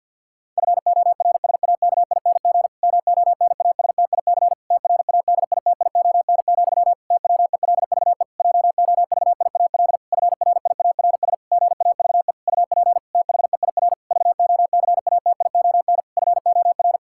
Morse code, heard as a keyed tone, about 52 wpm, 700 Hz. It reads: FORSAKING MONASTIC TRADITION, TWELVE JOVIAL FRIARS GAVE UP THEIR VOCATION FOR
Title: Pangram Challenge v2 - 50wpm